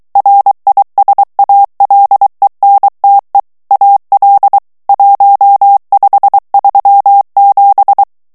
Morse code, heard as a keyed tone, about 23 wpm, 800 Hz. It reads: RISALENTE AL 1537
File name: ITALIEN texte clair 1 vitesse 22 mots minute